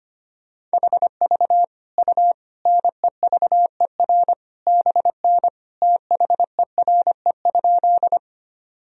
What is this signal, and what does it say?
Morse code, heard as a keyed tone, about 25 wpm, 700 Hz. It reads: HV U NEVER BN THERE?
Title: Sentences from Top 300 Words and Abbreviations 25wpm